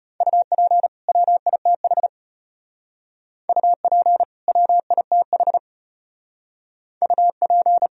Morse code, heard as a keyed tone, about 32 wpm, 700 Hz. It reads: UP WITH UP WITH UP